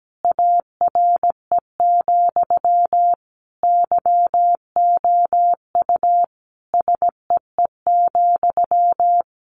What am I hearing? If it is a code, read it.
Morse code, heard as a keyed tone, about 17 wpm, 700 Hz. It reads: ARE, YOU SEE,
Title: Sentences from Top 1200 Words 17wpm